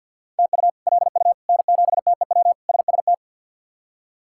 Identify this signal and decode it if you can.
Morse code, heard as a keyed tone, about 51 wpm, 700 Hz. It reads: TU LU N7TEW SST
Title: K1USN SST QSO Practice - 50wpm